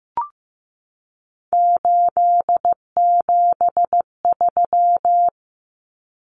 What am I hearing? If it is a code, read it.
Morse code, heard as a keyed tone, about 15 wpm, 700 Hz. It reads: 873
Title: Common Cut Numbers - 3-Digit 15wpm